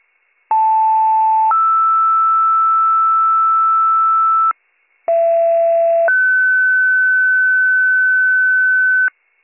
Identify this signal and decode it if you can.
Two-tone page, heard as a one-second tone, then a three-second tone, about 880 then 1320 Hz, about 670 then 1500 Hz, with noise. It is heard from 0.5 to 9.1 s.